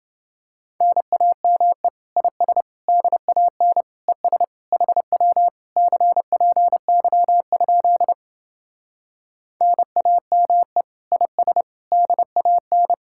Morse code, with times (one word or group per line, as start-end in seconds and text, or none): 0.8-1.9 NAME
2.2-2.6 IS
2.9-3.8 DAN
4.1-4.4 ES
4.7-5.5 HW
5.8-8.1 CPY?
9.6-10.8 NAME
11.1-11.6 IS
11.9-12.9 DAN